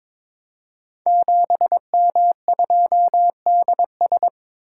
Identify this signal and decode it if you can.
Morse code, heard as a keyed tone, about 22 wpm, 700 Hz. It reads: 7M2DS
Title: Code Groups - 5 Letter:Numbers - 22wpm